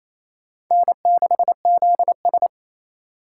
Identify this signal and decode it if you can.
Morse code, heard as a keyed tone, about 28 wpm, 700 Hz. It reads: N6ZS
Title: USA Call Signs - 8x Character Spacing - 28wpm